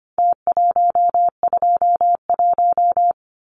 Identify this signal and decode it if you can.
Morse code, heard as a keyed tone, about 25 wpm, 700 Hz. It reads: T121
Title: Numbers - Contest Style - 4-Digit 25wpm